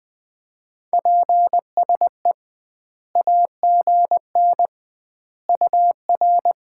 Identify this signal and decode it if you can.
Morse code, heard as a keyed tone, about 21 wpm, 700 Hz. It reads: PSE AGN UR
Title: Common QSO Phrases - 2x Word-Spacing - 20wpm